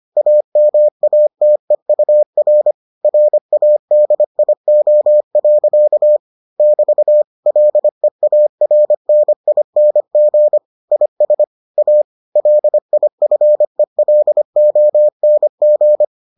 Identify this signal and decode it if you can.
Morse code, heard as a keyed tone, about 25 wpm, 600 Hz. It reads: AMATEUR RADIO. <BT> LEARNING IS A LIFELONG